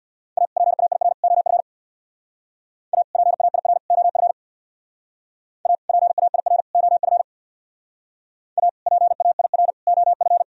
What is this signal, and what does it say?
Morse code, heard as a keyed tone, about 64 wpm, 700 Hz. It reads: A PAIR OF A PAIR OF A PAIR OF A PAIR OF